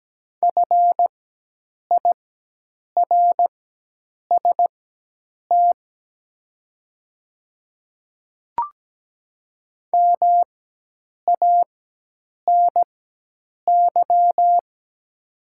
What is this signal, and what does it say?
Morse code, heard as a keyed tone, about 17 wpm, spaced as for 8 wpm, 700 Hz. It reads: FIRST MANY
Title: 4x Character Spacing - Top 200 Words - 17wpm